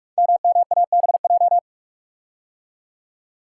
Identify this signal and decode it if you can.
Morse code, heard as a keyed tone, about 43 wpm, 700 Hz. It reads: MMABJ
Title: Code Groups - 5 Letter:Numbers - 45wpm